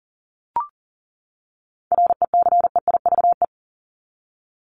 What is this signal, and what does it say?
Morse code, heard as a keyed tone, about 40 wpm, 700 Hz. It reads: RECEIVE
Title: ING Suffix - Top 100 Words with Base Words 40wpm